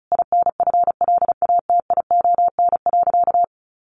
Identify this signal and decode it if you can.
Morse code, heard as a keyed tone, about 34 wpm, 700 Hz. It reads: INFLATION.
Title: Sentences from Top 1100 Words 35wpm